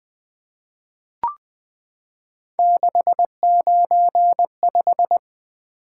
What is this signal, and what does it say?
Morse code, heard as a keyed tone, about 20 wpm, 700 Hz. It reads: 695
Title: Three Letter-Number Combinations - 20wpm